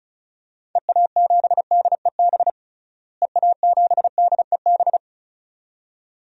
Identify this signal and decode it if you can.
Morse code, heard as a keyed tone, about 36 wpm, 700 Hz. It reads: EA7DEB EA7DEB
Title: DXCC-100 Call Signs - 6 Characters - Repeated 2x - 35wpm